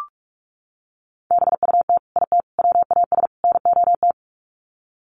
Morse code, heard as a keyed tone, about 44 wpm, 700 Hz. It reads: BUT IT WAS NOT
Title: Sets of 4 Words 45wpm